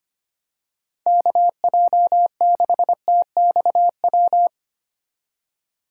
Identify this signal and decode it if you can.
Morse code, heard as a keyed tone, about 25 wpm, 700 Hz. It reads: KJ6TXW
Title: USA Call Signs - 6 Characters - 2x Character Spacing - 25wpm